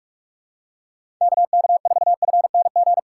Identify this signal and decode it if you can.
Morse code, heard as a keyed tone, about 41 wpm, 700 Hz. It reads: KK4FNG
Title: USA Call Signs - 6 Characters - 4x Character Spacing - 45wpm